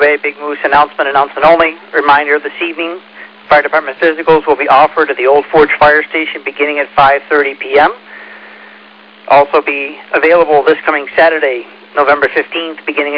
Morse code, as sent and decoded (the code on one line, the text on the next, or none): none